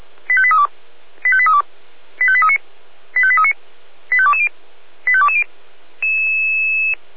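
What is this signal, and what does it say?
Five-tone selective call (ZVEI), heard as quick runs of five tones, tones of about 70 ms. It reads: 85832 85832 85839 85839 85220 85220
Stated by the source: ZVEI five-tone call 85830 received by SDR, from audio.f85360.feuerwehr-orlamuende.org, fire brigade Orlamünde.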